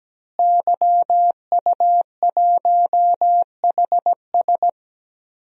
Morse code, heard as a keyed tone, about 17 wpm, 700 Hz. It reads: YU1HS
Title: DXCC-100 Call Signs - 5 Characters - Repeated 2x - 17wpm